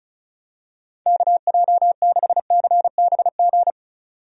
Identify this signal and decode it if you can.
Morse code, heard as a keyed tone, about 36 wpm, 700 Hz. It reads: KJ6CBG
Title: USA Call Signs - 6 Characters - 35wpm